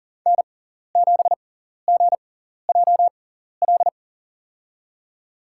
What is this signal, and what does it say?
Morse code, heard as a keyed tone, about 40 wpm, spaced as for 14 wpm, 700 Hz. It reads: N7GJL
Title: USA Call Signs - 5 Characters - 6x Character Spacing - 40wpm